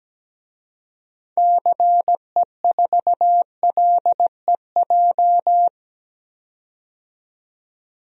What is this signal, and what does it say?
Morse code, heard as a keyed tone, about 17 wpm, 700 Hz. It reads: CE4LEJ
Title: DXCC-100 Call Signs - 6 Characters - 17wpm